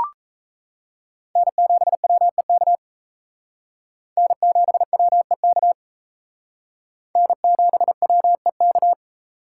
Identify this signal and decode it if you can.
Morse code, heard as a keyed tone, about 35 wpm, 700 Hz. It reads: N7WEK N7WEK N7WEK
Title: Speed Racing - USA 1x3 Call Signs 28wpm